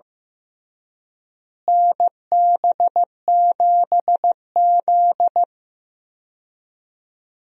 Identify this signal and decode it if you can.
Morse code, heard as a keyed tone, about 15 wpm, 700 Hz. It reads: NB7Z
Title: USA Call Signs - 4 Characters - Repeated 3x - 15wpm